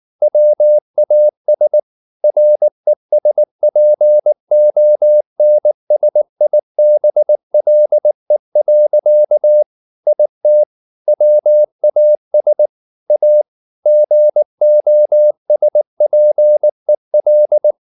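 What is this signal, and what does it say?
Morse code, heard as a keyed tone, about 19 wpm, 600 Hz. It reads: WAS RESPONSIBLE. IT WAS A GOSPEL
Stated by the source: Morse code at 19 wpm from CW med Gnister, reading Ken Follett's novel Never in English.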